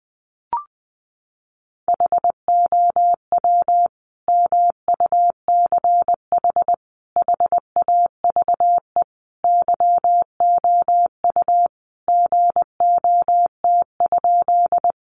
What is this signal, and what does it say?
Morse code, heard as a keyed tone, about 20 wpm, 700 Hz. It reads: HOW MUCH HAVE YOU GOT?
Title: Sentences from Top 300 Words 20wpm